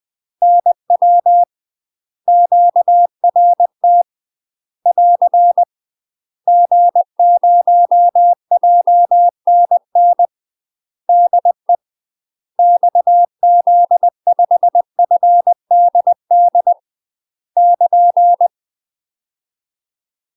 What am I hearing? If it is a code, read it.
Morse code, heard as a keyed tone, about 20 wpm, 700 Hz. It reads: NW QRT <AR> G0JNN DE XZ5FDD <KN>
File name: UK MASTER QSOs TRAINING / 20WPM / UK MASTER QSOs 0049